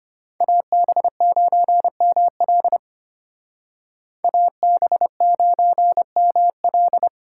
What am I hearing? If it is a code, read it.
Morse code, heard as a keyed tone, about 32 wpm, 700 Hz. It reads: AB9ML AB9ML